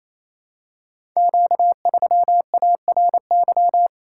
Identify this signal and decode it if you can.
Morse code, heard as a keyed tone, about 27 wpm, 700 Hz. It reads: Q3ARY
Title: Code Groups - 5 Letter:Numbers - 28wpm